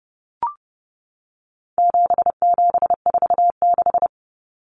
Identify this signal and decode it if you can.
Morse code, heard as a keyed tone, about 30 wpm, 700 Hz. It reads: 7746
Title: Numbers - 4-Digit 30wpm